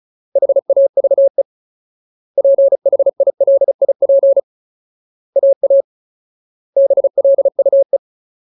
Morse code, heard as a keyed tone, about 34 wpm, 550 Hz. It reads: HAVE PHILIP AA BLUE
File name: Ultimate Word List / 35WPM Ultimate